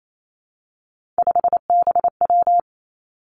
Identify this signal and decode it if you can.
Morse code, heard as a keyed tone, about 28 wpm, 700 Hz. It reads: 5BW
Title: Three Challenging Letter-Number Combinations 28wpm